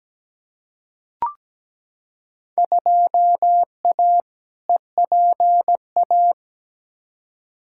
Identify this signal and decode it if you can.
Morse code, heard as a keyed tone, about 17 wpm, 700 Hz. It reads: 2A EPA
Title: Field Day Exchanges - 17wpm